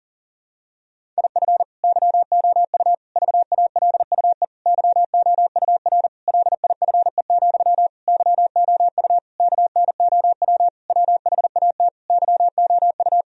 Morse code, heard as a keyed tone, about 40 wpm, 700 Hz. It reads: IF YOU VALUE YOUR LIFE, YOU KNOW WHAT YOU
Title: Sentences from Top 1000 Words - Review Part 3 - 40wpm